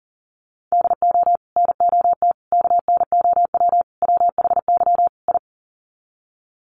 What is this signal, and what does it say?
Morse code, heard as a keyed tone, about 37 wpm, 700 Hz. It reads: DO NOT KNOW WHY I
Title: Sets of 5 Words Encore 40wpm